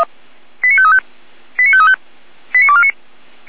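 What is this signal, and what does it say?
Five-tone selective call (ZVEI), heard as quick runs of five tones, tones of about 70 ms. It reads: 89535 89535 89249
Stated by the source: Five-tone ZVEI selective call received on SDR.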